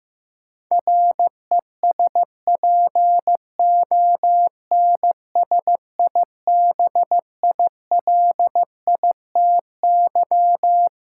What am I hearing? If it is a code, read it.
Morse code, heard as a keyed tone, about 15 wpm, 700 Hz. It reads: RESPONSIBILITY
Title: Sentences from Top 1200 Words - New Word Drill - 15wpm